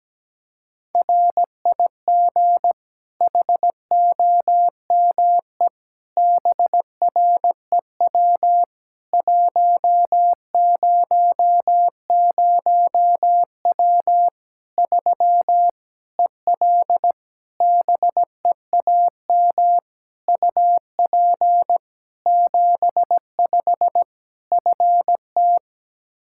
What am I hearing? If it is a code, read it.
Morse code, heard as a keyed tone, about 17 wpm, 700 Hz. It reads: RIG HOME BREW 100W 3 EL BEAM UP 75 FT
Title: K1USN SST QSO Practice 17wpm